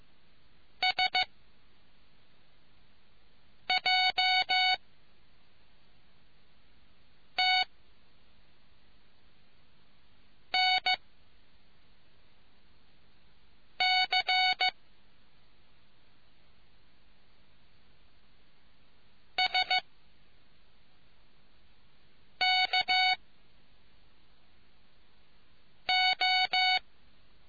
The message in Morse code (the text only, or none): S J T N C S K O